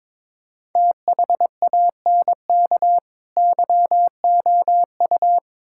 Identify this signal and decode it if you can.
Morse code, heard as a keyed tone, about 23 wpm, 700 Hz. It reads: THANK YOU
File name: practice / 22wpm / Build A Phrase - 5-Word Phrases - 22wpm